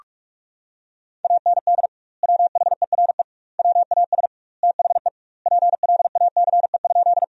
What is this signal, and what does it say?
Morse code, heard as a keyed tone, about 46 wpm, 700 Hz. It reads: AND WHERE WAS THE PLACE?
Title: 2x Word-Spacing - Sentences from Top 200 Words - 45wpm